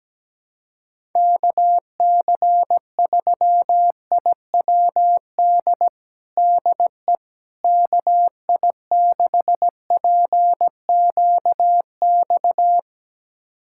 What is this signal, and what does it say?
Morse code, heard as a keyed tone, about 17 wpm, 700 Hz. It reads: KC3IWD DE KI6PQX
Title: USA Callsign DE Callsign - 6 Characters - 17wpm